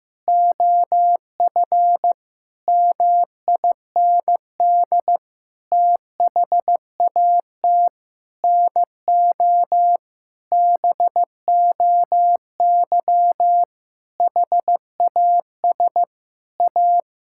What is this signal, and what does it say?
Morse code, heard as a keyed tone, about 15 wpm, 700 Hz. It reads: OF MIND THAT NO BOY HAS A
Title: Sentences from Top 1000 Words - Review Part 6 - 15wpm